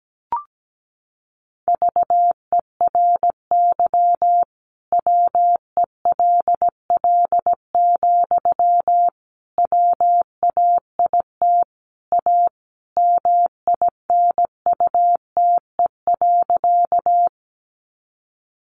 Morse code, heard as a keyed tone, about 17 wpm, 700 Hz. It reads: VERY WELL, WAIT A MINUTE.
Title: Sentences from Top 500 Words 17wpm